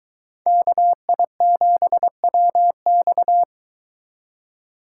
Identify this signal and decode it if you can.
Morse code, heard as a keyed tone, about 23 wpm, 700 Hz. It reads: KI7WX